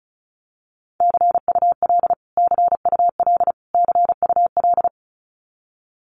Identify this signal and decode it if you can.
Morse code, heard as a keyed tone, about 34 wpm, 700 Hz. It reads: CUL CUL CUL
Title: Abbreviations Repeated 3x - 35wpm